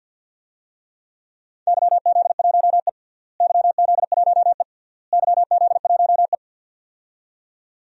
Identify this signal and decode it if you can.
Morse code, heard as a keyed tone, about 52 wpm, 700 Hz. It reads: YZ1E YZ1E YZ1E
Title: DXCC-100 Call Signs - 4 Characters - Repeated 3x - 50wpm